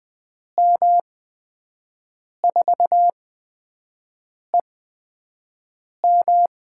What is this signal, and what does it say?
Morse code, heard as a keyed tone, about 20 wpm, spaced as for 5 wpm, 700 Hz. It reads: M4EM
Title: USA Call Signs - 8x Character Spacing - 20wpm